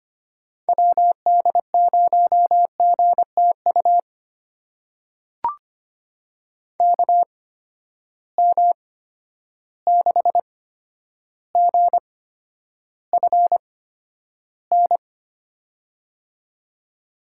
Morse code, heard as a keyed tone, about 25 wpm, spaced as for 11 wpm, 700 Hz. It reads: WD0GTU KM6GFN
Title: USA Call Signs - 6 Characters - 8x Character Spacing - 25wpm